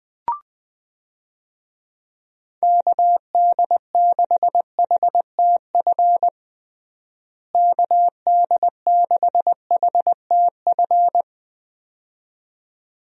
Morse code, heard as a keyed tone, about 21 wpm, 700 Hz. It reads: KD6HTF KD6HTF
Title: USA Call Signs - 6 Characters - Repeated 2x - 20wpm